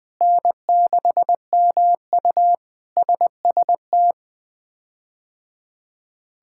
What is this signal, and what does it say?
Morse code, heard as a keyed tone, about 20 wpm, 700 Hz. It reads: N6MU SST
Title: K1USN SST QSO Practice - 20wpm